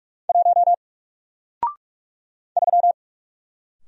Morse code, heard as a keyed tone, about 44 wpm, 700 Hz. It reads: 1 3
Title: Single Character (Letter:Number:Punctuation:Prosign) Alt 45wpm